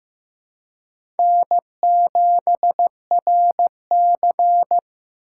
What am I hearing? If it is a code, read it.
Morse code, heard as a keyed tone, about 15 wpm, 700 Hz. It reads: N7RC
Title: USA Call Signs - 4 Characters - 2x Character Spacing - 15wpm